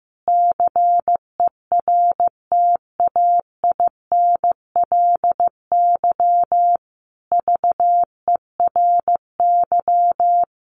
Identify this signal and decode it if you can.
Morse code, heard as a keyed tone, about 15 wpm, 700 Hz. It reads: CERTAINLY VERY
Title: Sentences from Top 1200 Words 15wpm